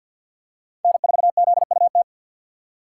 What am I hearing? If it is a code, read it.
Morse code, heard as a keyed tone, about 53 wpm, 700 Hz. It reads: N4ZUT
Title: USA Call Signs - 5 Characters - 4x Character Spacing - 50wpm